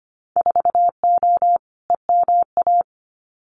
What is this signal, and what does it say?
Morse code, heard as a keyed tone, about 25 wpm, 700 Hz. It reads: 4O EMA
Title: Winter Field Day Exchanges Encore 25wpm